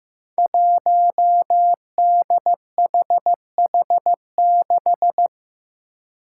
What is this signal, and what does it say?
Morse code, heard as a keyed tone, about 15 wpm, 700 Hz. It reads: 1DHH6
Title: Five Challenging Letter-Number Combinations 15wpm